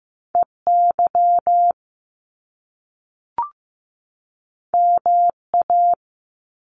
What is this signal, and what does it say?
Morse code, heard as a keyed tone, about 15 wpm, 700 Hz. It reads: EY MA
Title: Top Two Letter Combinations in Words 15wpm